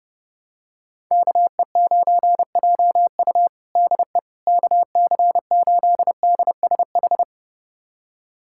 Morse code, heard as a keyed tone, about 32 wpm, 700 Hz. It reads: KE9JU DE KC8DSH
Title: USA Callsign DE Callsign 30wpm